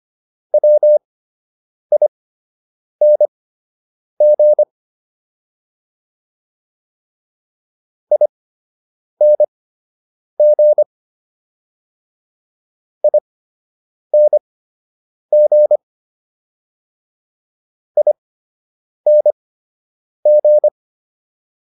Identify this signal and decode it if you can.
Morse code, heard as a keyed tone, about 25 wpm, spaced as for 8 wpm, 600 Hz. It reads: WING ING ING ING